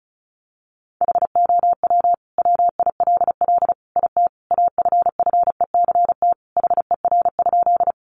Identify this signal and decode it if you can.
Morse code, heard as a keyed tone, about 34 wpm, 700 Hz. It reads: HOW WILL IT AFFECT HER?
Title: Sentences from Top 900 Words 35wpm